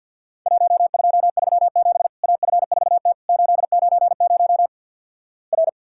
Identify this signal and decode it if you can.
Morse code, heard as a keyed tone, about 48 wpm, 700 Hz. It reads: 1237 AF4T 890 R
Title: Worked All Europe DX Contest - QTC Exchange with Repeats - 50wpm